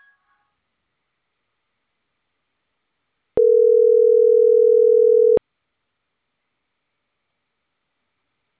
Ringback tone (North America), 440 + 480 Hz.